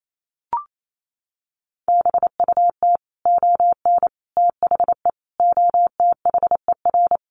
Morse code, heard as a keyed tone, about 28 wpm, 700 Hz. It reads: BUT ON THE OTHER
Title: Sets of 5 Words 28wpm